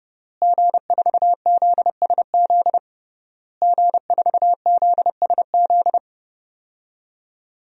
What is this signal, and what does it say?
Morse code, heard as a keyed tone, about 29 wpm, 700 Hz. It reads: G4ZSZ G4ZSZ
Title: DXCC-100 Call Signs - 5 Characters - Repeated 3x - 30wpm